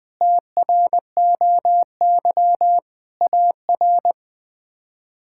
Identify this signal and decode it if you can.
Morse code, heard as a keyed tone, about 21 wpm, 700 Hz. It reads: TROY AR
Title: K1USN SST QSO Practice - 20wpm